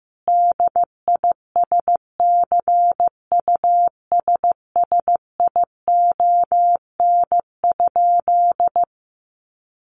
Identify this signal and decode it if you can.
Morse code, heard as a keyed tone, about 15 wpm, 700 Hz. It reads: DISCUSSION?
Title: Sentences from Top 1100 Words 15wpm